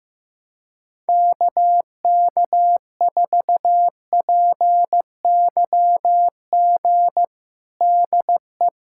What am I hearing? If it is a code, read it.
Morse code, heard as a keyed tone, about 15 wpm, 700 Hz. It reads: KK4PYG DE
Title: USA Callsign DE Callsign 15wpm